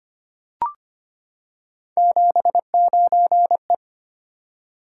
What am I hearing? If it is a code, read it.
Morse code, heard as a keyed tone, about 25 wpm, 700 Hz. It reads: 79E
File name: Three Letter-Number Combinations - 25wpm